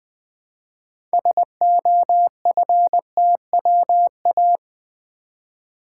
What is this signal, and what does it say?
Morse code, heard as a keyed tone, about 20 wpm, 700 Hz. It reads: SOFTWA
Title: Build A Word - 3 Letters At A Time - 8-Letter Words - 20wpm